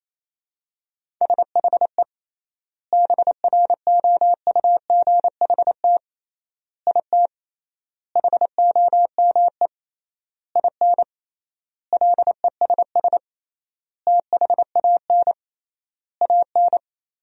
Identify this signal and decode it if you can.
Morse code, heard as a keyed tone, about 28 wpm, 700 Hz. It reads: SHE BROUGHT IT HOME IN LESS THAN AN
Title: 3x Word-Spacing - Sentences from Top 500 Words - 28wpm